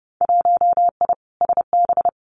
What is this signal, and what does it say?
Morse code, heard as a keyed tone, about 30 wpm, 700 Hz. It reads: 1I SB
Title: Winter Field Day Exchanges 30wpm